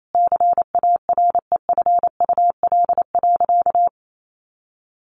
Morse code, heard as a keyed tone, about 28 wpm, 700 Hz. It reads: CAREFUL.
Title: A Sound Of Thunder 28wpm